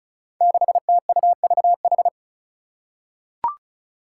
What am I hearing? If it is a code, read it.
Morse code, heard as a keyed tone, about 34 wpm, 700 Hz. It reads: 6TUVH
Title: Five Challenging Letter-Number Combinations 35wpm